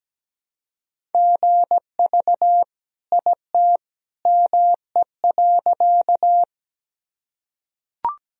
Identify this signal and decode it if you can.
Morse code, heard as a keyed tone, about 17 wpm, 700 Hz. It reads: GV IT ME.
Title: Sentences from Top 200 Words and Abbreviations 17wpm